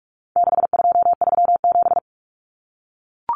Morse code, heard as a keyed tone, about 44 wpm, 700 Hz. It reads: 6237
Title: Numbers - 4-Digit 45wpm